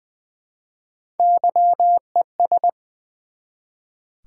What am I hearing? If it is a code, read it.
Morse code, heard as a keyed tone, about 20 wpm, 700 Hz. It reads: YES
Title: Build A Word - 1 Letter At A Time - 3-Letter Words - 20wpm